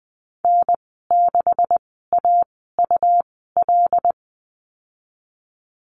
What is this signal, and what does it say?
Morse code, heard as a keyed tone, about 20 wpm, spaced as for 15 wpm, 700 Hz. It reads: N6AUL
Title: 173 Lesson 28g - Number 9 - Call Signs 20wpm